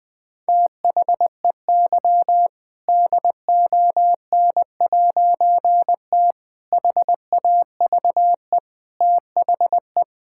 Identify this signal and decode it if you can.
Morse code, heard as a keyed tone, about 20 wpm, 700 Hz. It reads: THEY DON'T HAVE THE
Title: All-Age Appropriate Jokes - 20wpm